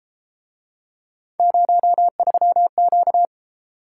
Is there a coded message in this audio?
Yes: Morse code from 1.4 to 3.2 s, heard as a keyed tone, about 33 wpm, 700 Hz. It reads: 03Q